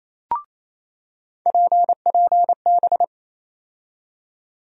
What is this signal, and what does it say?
Morse code, heard as a keyed tone, about 28 wpm, 700 Hz. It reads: PPB
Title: USA Call Sign Postfixes - 3 Letters 28wpm